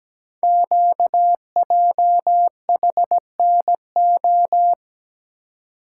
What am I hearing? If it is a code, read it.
Morse code, heard as a keyed tone, about 17 wpm, 700 Hz. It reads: QJHNO
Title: Code Groups - 5 Letters - 17wpm